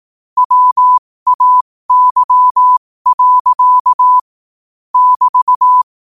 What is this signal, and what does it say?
Morse code, heard as a keyed tone, about 18 wpm, spaced as for 15 wpm, 1000 Hz. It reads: WAY. <BT>
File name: qotdfiles / 15wpm